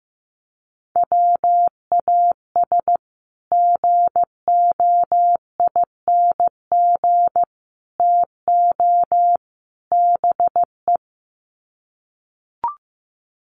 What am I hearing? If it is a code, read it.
Morse code, heard as a keyed tone, about 15 wpm, 700 Hz. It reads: WAS GOING TO BE
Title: Sets of 4 Words 15wpm